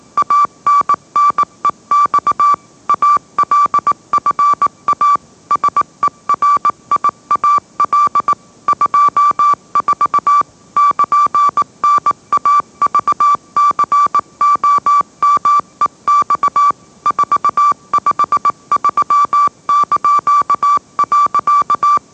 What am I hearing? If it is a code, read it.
Morse code, heard as a keyed tone, about 18 wpm, 1160 Hz. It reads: ANNEX ALFA SERIAL 24 <KN>NAVCOMEX 453).
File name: mrx / mrx081 20wpm